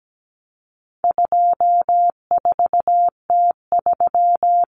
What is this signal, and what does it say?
Morse code, heard as a keyed tone, about 17 wpm, 700 Hz. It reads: 24T3
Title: Common Cut Numbers - 4-Digit 17wpm